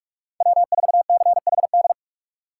Morse code, heard as a keyed tone, about 44 wpm, 700 Hz. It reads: W4CHD
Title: USA Call Signs - 5 Characters - 6x Character Spacing - 45wpm